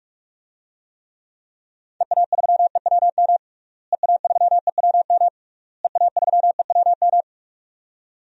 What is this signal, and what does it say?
Morse code, heard as a keyed tone, about 46 wpm, 700 Hz. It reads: EA3EWM EA3EWM EA3EWM
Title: DXCC-100 Call Signs - 6 Characters - Repeated 3x - 45wpm